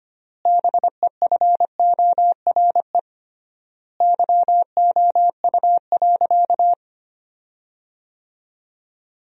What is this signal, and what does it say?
Morse code, heard as a keyed tone, about 25 wpm, 700 Hz. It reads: BEFORE YOU.
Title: 3x Word-Spacing - Sentences from Top 200 Words - 25wpm